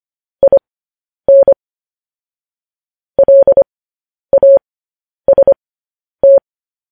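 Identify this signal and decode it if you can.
Morse code, heard as a keyed tone, about 25 wpm, spaced as for 10 wpm, 550 Hz. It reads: IN LAST
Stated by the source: Morse code news headlines.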